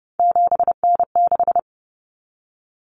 Morse code, heard as a keyed tone, about 30 wpm, 700 Hz. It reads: 7N6
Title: Common Cut Numbers - 3-Digit 30wpm